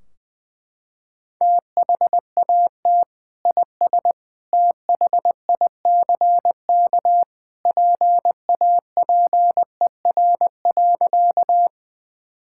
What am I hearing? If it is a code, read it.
Morse code, heard as a keyed tone, about 21 wpm, 700 Hz. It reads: THAT IS THICK PAPER.
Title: Sentences from Top 1600 Words 20wpm